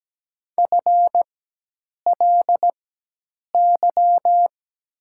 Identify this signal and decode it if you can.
Morse code, heard as a keyed tone, about 17 wpm, spaced as for 8 wpm, 700 Hz. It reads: FLY
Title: 4x Character Spacing - Top 500 Words - 17wpm